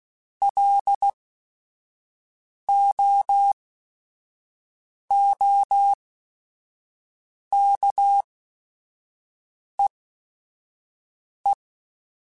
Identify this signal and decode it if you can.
Morse code, heard as a keyed tone, about 16 wpm, spaced as for 5 wpm, 800 Hz. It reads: LOOKEE